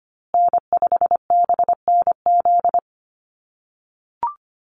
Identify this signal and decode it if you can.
Morse code, heard as a keyed tone, about 25 wpm, 700 Hz. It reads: N5BNZ
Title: USA Call Signs Repeated 3x 25wpm